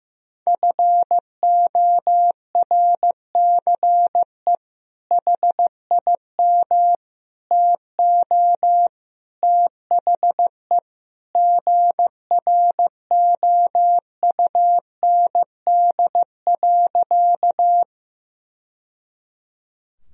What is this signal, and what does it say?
Morse code, heard as a keyed tone, about 15 wpm, 700 Hz. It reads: FORCE HIM TO THE GROUND.
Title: Sentences from Top 1000 Words - Review Part 3 - 15wpm